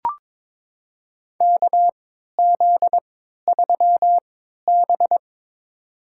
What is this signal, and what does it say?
Morse code, heard as a keyed tone, about 22 wpm, spaced as for 13 wpm, 700 Hz. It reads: KZ3B
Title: USA Call Signs - 4 Characters - 3x Character Spacing - 22wpm